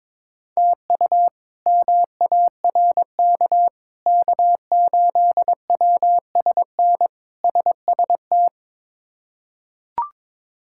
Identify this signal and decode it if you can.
Morse code, heard as a keyed tone, about 22 wpm, 700 Hz. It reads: TU MARK K8WSN SST
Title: K1USN SST QSO Practice - 22wpm